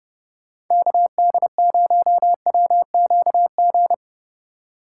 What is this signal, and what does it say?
Morse code, heard as a keyed tone, about 31 wpm, 700 Hz. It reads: KD0WQG
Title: USA Call Signs - 6 Characters - 3x Character Spacing - 30wpm